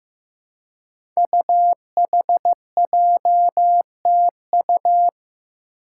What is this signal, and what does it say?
Morse code, heard as a keyed tone, about 15 wpm, 700 Hz. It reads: UHJTU
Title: Five Challenging Letter-Number Combinations 15wpm